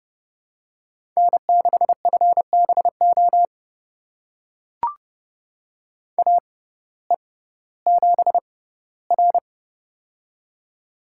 Morse code, heard as a keyed tone, about 30 wpm, spaced as for 17 wpm, 700 Hz. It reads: N6FBO AE7R
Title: USA Call Signs - 6x Character Spacing - 30wpm